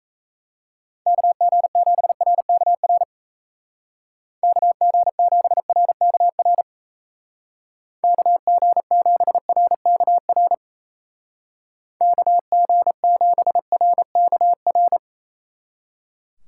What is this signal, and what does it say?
Morse code, heard as a keyed tone, about 33 wpm, 700 Hz. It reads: KG7RKR KG7RKR KG7RKR KG7RKR
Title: Speed Racing - USA 2x3 Call Signs 28wpm